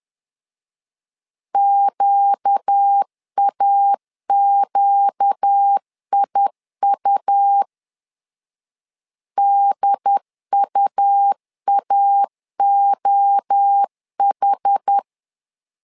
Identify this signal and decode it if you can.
Morse code, heard as a keyed tone, about 10 wpm, 800 Hz. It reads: QAQIU DUAOH
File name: LEZIONI / 4 Q 10wpm 2210